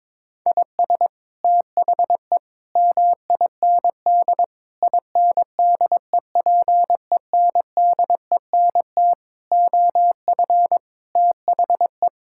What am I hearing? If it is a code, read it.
Morse code, heard as a keyed tone, about 22 wpm, 700 Hz. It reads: IS THE MIND INDEPENDENT OF THE
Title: Sentences from Top 1000 Words - Review Part 6 - 22wpm